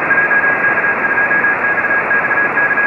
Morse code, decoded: JK2XXK